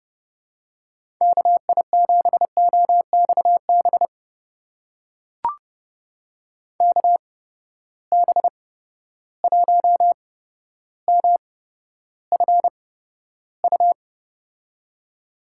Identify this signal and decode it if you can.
Morse code, heard as a keyed tone, about 30 wpm, spaced as for 13 wpm, 700 Hz. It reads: KI7OXB KB1MFU
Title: USA Call Signs - 6 Characters - 8x Character Spacing - 30wpm